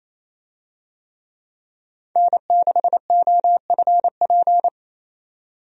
Morse code, heard as a keyed tone, about 28 wpm, 700 Hz. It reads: N6OFP
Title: USA Call Signs - 5 Characters - 28wpm